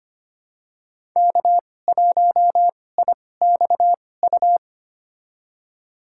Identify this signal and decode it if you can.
Morse code, heard as a keyed tone, about 25 wpm, spaced as for 18 wpm, 700 Hz. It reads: K1IXU
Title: USA Call Signs - 5 Characters - 2x Character Spacing - 25wpm